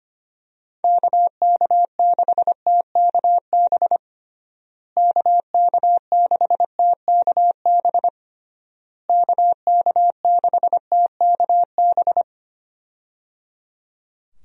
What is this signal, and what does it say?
Morse code, heard as a keyed tone, about 25 wpm, 700 Hz. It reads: KK6TKB KK6TKB KK6TKB
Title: USA Call Signs - 6 Characters - Repeated 3x - 25wpm